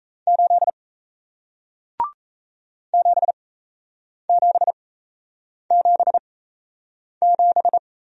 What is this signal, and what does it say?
Morse code, heard as a keyed tone, about 37 wpm, spaced as for 8 wpm, 700 Hz. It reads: 8 7777